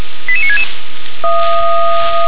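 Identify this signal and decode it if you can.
Five-tone selective call (ZVEI), heard as quick runs of five tones, tones of about 70 ms. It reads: 88966
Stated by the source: Five-tone ZVEI selective call received on an SDR.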